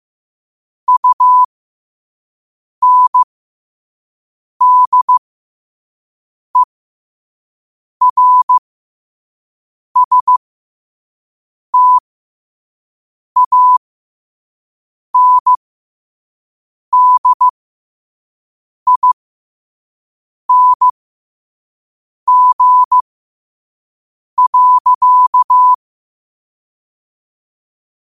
Morse code, heard as a keyed tone, about 15 wpm, spaced as for 5 wpm, 1000 Hz. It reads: UNDERSTANDING.
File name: qotdfiles / 5wpm